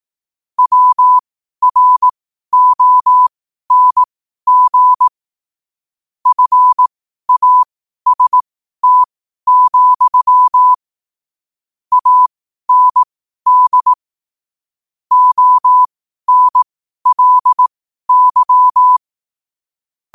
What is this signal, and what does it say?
Morse code, heard as a keyed tone, about 18 wpm, spaced as for 12 wpm, 1000 Hz. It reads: WRONG FAST, AND ONLY
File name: qotdfiles / 12wpm